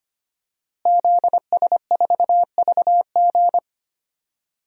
Morse code, heard as a keyed tone, about 25 wpm, 700 Hz. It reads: ZS4VG
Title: DXCC-100 Call Signs - 5 Characters - Repeated 3x - 25wpm